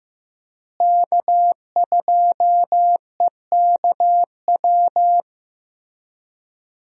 Morse code, heard as a keyed tone, about 15 wpm, 700 Hz. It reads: K2EKW
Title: USA Call Signs - 5 Characters - Repeated 3x - 15wpm